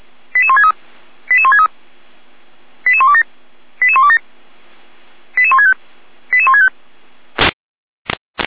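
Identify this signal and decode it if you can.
Five-tone selective call (ZVEI), heard as quick runs of five tones, tones of about 70 ms. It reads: 88163 88163 88127 88127 88165 88165
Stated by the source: ZVEI five-tone selective call, received on SDR.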